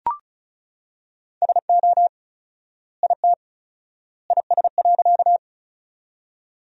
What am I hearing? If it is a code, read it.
Morse code, heard as a keyed tone, about 34 wpm, 700 Hz. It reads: SO IT IS.
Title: 4x Word-Spacing - Sentences from Top 100 Words - 35wpm